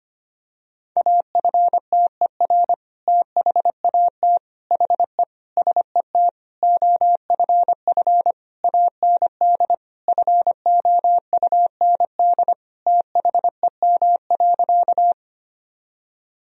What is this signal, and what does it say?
Morse code, heard as a keyed tone, about 25 wpm, 700 Hz. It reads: AFTER THAT HE SET OFF AND FOUND THEM.
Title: Sentences from Top 1000 Words - Review Part 1 - 25wpm